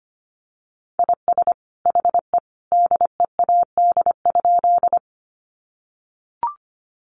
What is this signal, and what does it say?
Morse code, heard as a keyed tone, about 25 wpm, 700 Hz. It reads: IS HE DEAD?
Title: Sentences from Top 1500 Words 25wpm